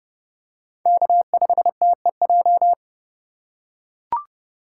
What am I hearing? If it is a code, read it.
Morse code, heard as a keyed tone, about 29 wpm, 700 Hz. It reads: K5TEJ